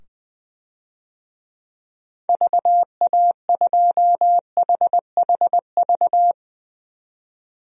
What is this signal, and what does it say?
Morse code, heard as a keyed tone, about 20 wpm, 700 Hz. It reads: VA2HHV
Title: DXCC-100 Call Signs - 6 Characters - 20wpm